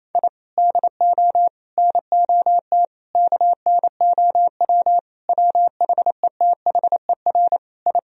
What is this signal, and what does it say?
Morse code, heard as a keyed tone, about 28 wpm, 700 Hz. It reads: I DO NOT KNOW WHETHER I
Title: Sentences from Top 1000 Words - Review Part 3 - 28wpm